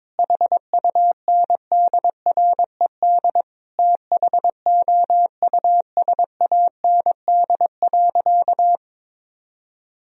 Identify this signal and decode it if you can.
Morse code, heard as a keyed tone, about 22 wpm, 700 Hz. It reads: HUNDRED THOUSAND.
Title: Sentences from Top 1000 Words - Review Part 3 - 22wpm